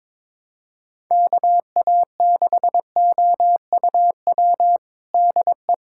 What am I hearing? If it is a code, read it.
Morse code, heard as a keyed tone, about 22 wpm, 700 Hz. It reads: KA6OUW DE
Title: USA Callsign DE Callsign - 6 Characters - 22wpm